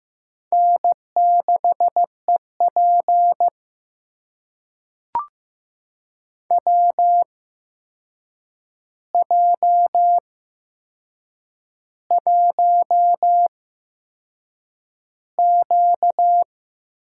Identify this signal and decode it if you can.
Morse code, heard as a keyed tone, about 15 wpm, spaced as for 6 wpm, 700 Hz. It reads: N6EP WJ1Q
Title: USA Call Signs - 8x Character Spacing - 15wpm